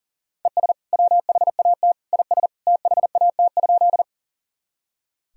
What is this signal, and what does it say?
Morse code, heard as a keyed tone, about 40 wpm, 700 Hz. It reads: ES WHAT IS THAT?
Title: Sentences from Top 100 Words and Abbreviations 40wpm